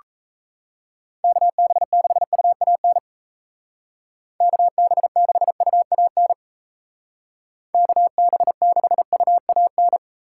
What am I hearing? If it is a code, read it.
Morse code, heard as a keyed tone, about 34 wpm, 700 Hz. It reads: KB6UAN KB6UAN KB6UAN
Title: Speed Racing - USA 2x3 Call Signs 28wpm